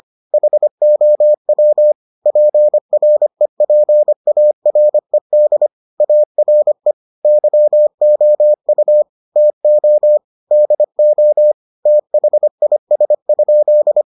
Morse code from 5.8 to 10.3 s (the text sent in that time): ARE YOU TO